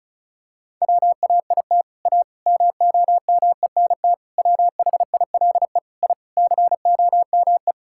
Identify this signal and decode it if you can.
Morse code, heard as a keyed tone, about 36 wpm, 700 Hz. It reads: WAIT A MOMENT WHILE I COME
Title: Sentences from Top 1000 Words - Review Part 5 - 35wpm